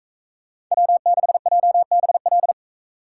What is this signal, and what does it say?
Morse code, heard as a keyed tone, about 41 wpm, 700 Hz. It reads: W6JBL